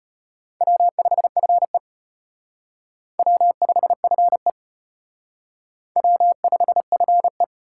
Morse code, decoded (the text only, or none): W5FE W5FE W5FE